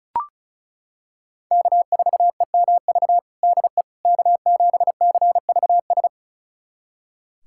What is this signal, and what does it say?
Morse code, heard as a keyed tone, about 34 wpm, 700 Hz. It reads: K4EMV DE K7CVS
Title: USA Callsign DE Callsign - 5 Characters - 35wpm